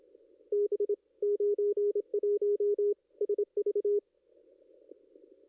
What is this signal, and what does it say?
Morse code, heard as a keyed tone, about 26 wpm, 410 Hz. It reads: B 91 SV
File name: contests / ss-cw-k5tr-op-n5rz -2200